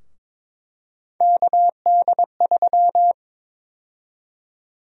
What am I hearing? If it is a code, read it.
Morse code, heard as a keyed tone, about 22 wpm, 700 Hz. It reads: KD3
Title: USA Call Sign Prefixes - Repeated 2x - 22wpm